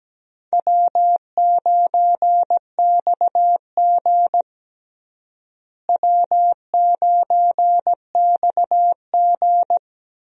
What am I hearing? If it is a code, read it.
Morse code, heard as a keyed tone, about 17 wpm, 700 Hz. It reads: W9XG W9XG
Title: USA Call Signs - 4 Characters - Repeated 2x - 17wpm